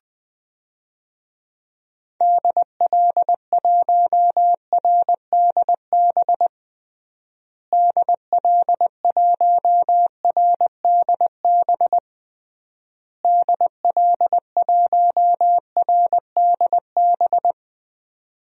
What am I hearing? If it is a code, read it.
Morse code, heard as a keyed tone, about 20 wpm, 700 Hz. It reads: DL1RDB DL1RDB DL1RDB
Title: DXCC-100 Call Signs - 6 Characters - Repeated 3x - 20wpm